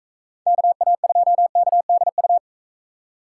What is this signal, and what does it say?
Morse code, heard as a keyed tone, about 44 wpm, 700 Hz. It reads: KA2KDU